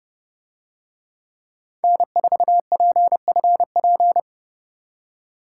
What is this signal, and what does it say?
Morse code, heard as a keyed tone, about 30 wpm, 700 Hz. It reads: N4PFP